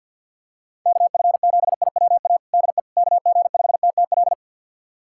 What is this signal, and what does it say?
Morse code, heard as a keyed tone, about 50 wpm, 700 Hz. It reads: KF7IWA DE KG5TTL
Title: USA Callsign DE Callsign 50wpm